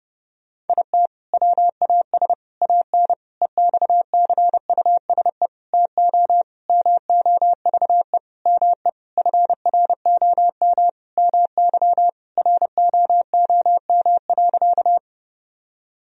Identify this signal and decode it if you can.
Morse code, heard as a keyed tone, about 32 wpm, 700 Hz. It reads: IT WAS AN EXCUSE TO MOVE ME FROM MY ROOM.
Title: Sentences from Top 1000 Words - Review Part 1 - 30wpm